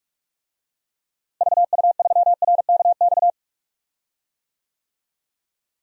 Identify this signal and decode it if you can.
Morse code, heard as a keyed tone, about 46 wpm, 700 Hz. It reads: VU3RKX
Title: DXCC-100 Call Signs - 6 Characters - Repeated 3x - 45wpm